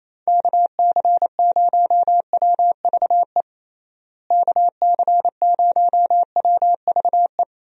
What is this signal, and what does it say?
Morse code, heard as a keyed tone, about 28 wpm, 700 Hz. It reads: KC0WVE KC0WVE
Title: USA Call Signs - 6 Characters - Repeated 2x - 28wpm